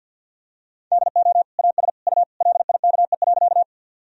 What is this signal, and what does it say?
Morse code, heard as a keyed tone, about 53 wpm, 700 Hz. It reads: DO AS U LIKE.
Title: Sentences from Top 100 Words and Abbreviations 50wpm